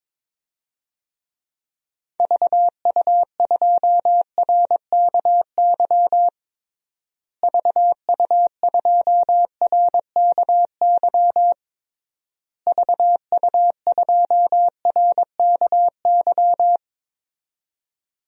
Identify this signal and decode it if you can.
Morse code, heard as a keyed tone, about 22 wpm, 700 Hz. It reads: VU2RKY VU2RKY VU2RKY
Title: DXCC-100 Call Signs - 6 Characters - Repeated 3x - 22wpm